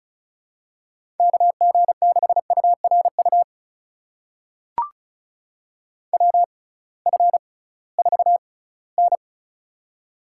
Morse code, heard as a keyed tone, about 36 wpm, spaced as for 20 wpm, 700 Hz. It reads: KG6URU WF4N
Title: USA Call Signs - 6x Character Spacing - 35wpm